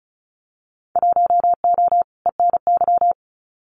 Morse code, heard as a keyed tone, about 36 wpm, 700 Hz. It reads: 1O ENY
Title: Winter Field Day Exchanges 35wpm